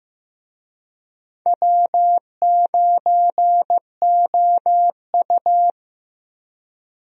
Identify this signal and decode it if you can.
Morse code, heard as a keyed tone, about 15 wpm, 700 Hz. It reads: W9OU
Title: USA Call Signs - 4 Characters - Repeated 2x - 15wpm